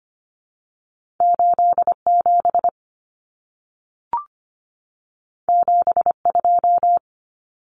Morse code, heard as a keyed tone, about 25 wpm, 700 Hz. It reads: 87 72
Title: Numbers - 2-Digit 25wpm